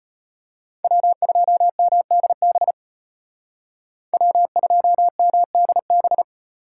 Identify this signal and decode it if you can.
Morse code, heard as a keyed tone, about 31 wpm, 700 Hz. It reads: W2MDB W2MDB
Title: Speed Racing - USA 1x3 Call Signs 25wpm